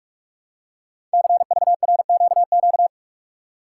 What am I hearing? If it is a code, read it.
Morse code, heard as a keyed tone, about 44 wpm, 700 Hz. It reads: CVRQQ
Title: Code Groups - 5 Letters - 45wpm